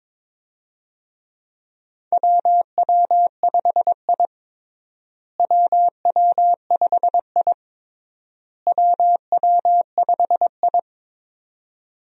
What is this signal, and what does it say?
Morse code, heard as a keyed tone, about 22 wpm, 700 Hz. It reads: WW5I WW5I WW5I
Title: USA Call Signs - 4 Characters - Repeated 3x - 22wpm